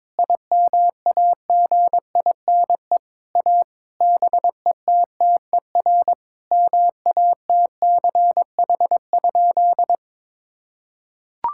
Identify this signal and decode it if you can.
Morse code, heard as a keyed tone, about 22 wpm, 700 Hz. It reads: IMAGINE A BETTER MATCH?
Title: Sentences from Top 1600 Words 22wpm